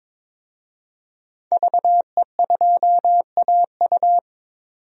Morse code, heard as a keyed tone, about 22 wpm, 700 Hz. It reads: VE2AU
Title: DXCC-100 Call Signs - 5 Characters - 22wpm